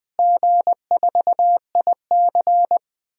Morse code, heard as a keyed tone, about 20 wpm, 690 Hz. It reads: G4IC
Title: DXCC-100 Call Signs - 4 Characters - Repeated 3x - 20wpm